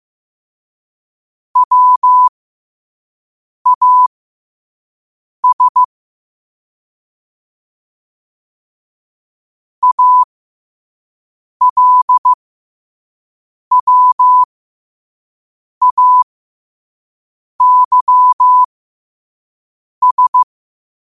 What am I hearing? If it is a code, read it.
Morse code, heard as a keyed tone, about 15 wpm, spaced as for 5 wpm, 1000 Hz. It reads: WAS ALWAYS
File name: qotdfiles / 5wpm